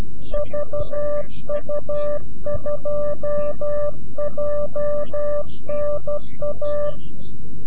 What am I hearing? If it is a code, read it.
Morse code, heard as a keyed tone, about 12 wpm, 590 Hz. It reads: VU2JNA
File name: CW-QSO / CQ Call / CQ-QSO